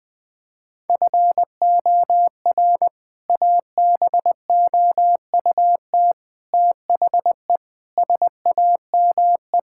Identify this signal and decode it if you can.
Morse code, heard as a keyed tone, about 19 wpm, 700 Hz. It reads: FOR ABOUT THE SAME
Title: Sentences from Top 1000 Words - Review Part 5 - 20wpm